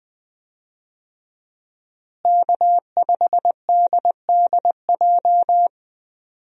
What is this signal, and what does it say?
Morse code, heard as a keyed tone, about 20 wpm, 700 Hz. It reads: K5DDJ
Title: USA Call Signs - 5 Characters - Repeated 2x - 20wpm